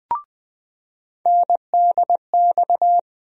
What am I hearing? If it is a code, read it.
Morse code, heard as a keyed tone, about 21 wpm, 700 Hz. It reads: NDX
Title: USA Call Sign Postfixes - 3 Letters 20wpm